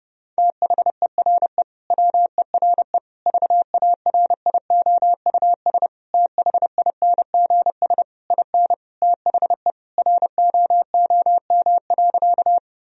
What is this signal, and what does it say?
Morse code, heard as a keyed tone, about 30 wpm, 700 Hz. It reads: THERE WERE VARIOUS THINGS IN THE ROOM.
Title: Sentences from Top 1000 Words - Review Part 4 - 30wpm